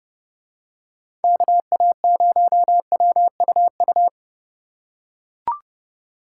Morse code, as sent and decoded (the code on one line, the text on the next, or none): -.- .- ----- .-- ..- ..-
KA0WUU